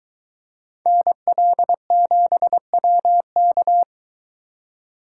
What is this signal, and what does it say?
Morse code, heard as a keyed tone, about 22 wpm, 700 Hz. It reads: NL7WK